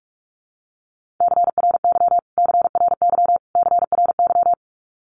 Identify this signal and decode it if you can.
Morse code, heard as a keyed tone, about 44 wpm, 700 Hz. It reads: CRY CRY CRY
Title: Top 500 Words Repeated 45wpm